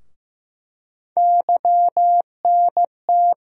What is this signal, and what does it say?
Morse code, heard as a keyed tone, about 15 wpm, 700 Hz. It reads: YNT
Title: USA Call Sign Postfixes - 3 Letters 15wpm